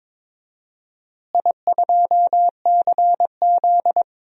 Morse code, heard as a keyed tone, about 22 wpm, 700 Hz. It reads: I2CZ
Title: DXCC-100 Call Signs - 4 Characters - Repeated 2x - 22wpm